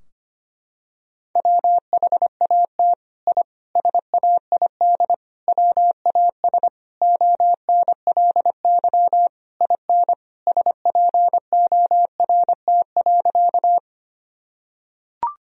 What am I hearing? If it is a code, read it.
Morse code, heard as a keyed tone, about 25 wpm, 700 Hz. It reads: WHAT I SAID WAS ONLY IN SPORT.
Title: Sentences from Top 1000 Words - Review Part 4 - 25wpm